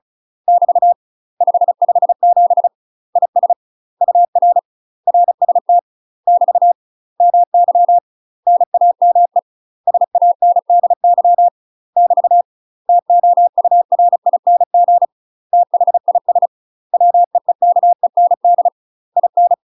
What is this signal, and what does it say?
Morse code, heard as a keyed tone, about 36 wpm, 700 Hz. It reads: <BT> 557 IS UR RST <BT> MY NAME SANDY <BT> TOURING THIS WEEKEND IN